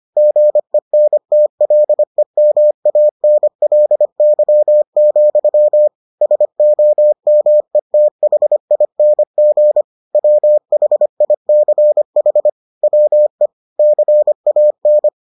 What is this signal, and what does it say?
Morse code, heard as a keyed tone, about 25 wpm, 600 Hz. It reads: GENTLEMANLY, SOMETHING WHICH WE CAN